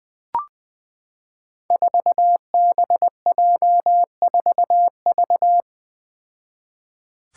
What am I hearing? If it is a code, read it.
Morse code, heard as a keyed tone, about 20 wpm, 700 Hz. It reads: 4BJ4V
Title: Five Challenging Letter-Number Combinations 20wpm